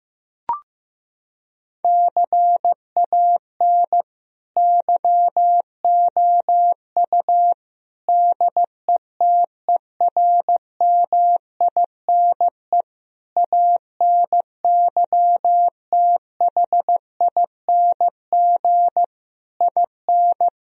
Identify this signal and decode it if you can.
Morse code, heard as a keyed tone, about 15 wpm, 700 Hz. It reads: CAN YOU DETERMINE ANYTHING IN
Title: Sentences from Top 1000 Words - Review Part 4 - 15wpm